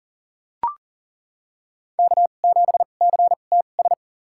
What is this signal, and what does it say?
Morse code, heard as a keyed tone, about 40 wpm, spaced as for 29 wpm, 700 Hz. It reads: K7CTS
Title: USA Call Signs - 5 Characters - 2x Character Spacing - 40wpm